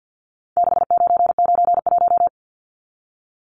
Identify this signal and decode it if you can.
Morse code, heard as a keyed tone, about 50 wpm, 700 Hz. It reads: 6991
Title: Numbers - 4-Digit 50wpm